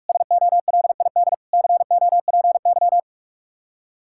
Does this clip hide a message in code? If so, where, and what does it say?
Yes: Morse code from 0.1 to 3.0 s, heard as a keyed tone, about 49 wpm, 700 Hz. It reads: SOLID COPY